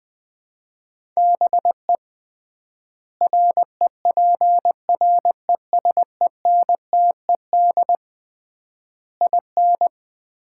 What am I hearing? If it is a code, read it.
Morse code, heard as a keyed tone, about 20 wpm, 700 Hz. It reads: BE REPRESENTED IN
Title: 3x Word-Spacing - Sets of 5 Words - 20wpm